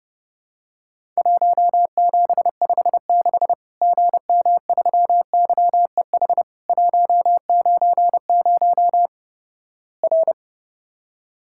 Morse code, heard as a keyed tone, about 31 wpm, 700 Hz. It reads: 1756 GM3YEH 190 R
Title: Worked All Europe DX Contest - QTC Exchange - 30wpm